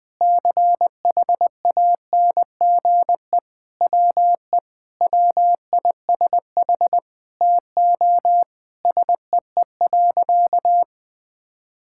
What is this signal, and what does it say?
Morse code, heard as a keyed tone, about 21 wpm, 700 Hz. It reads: CHANGE WE WISH TO SEE.
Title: Sentences with 8-Word Phrases - 20wpm